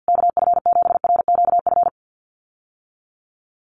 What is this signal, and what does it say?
Morse code, heard as a keyed tone, about 50 wpm, 700 Hz. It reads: KF7RQF
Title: USA Call Signs Repeated 2x 50wpm